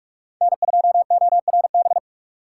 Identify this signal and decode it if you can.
Morse code, heard as a keyed tone, about 45 wpm, 700 Hz. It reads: N2ORB